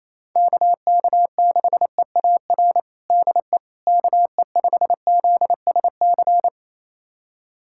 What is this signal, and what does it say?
Morse code, heard as a keyed tone, about 28 wpm, 700 Hz. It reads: KK6EAR DE KE5ZSC
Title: USA Callsign DE Callsign 28wpm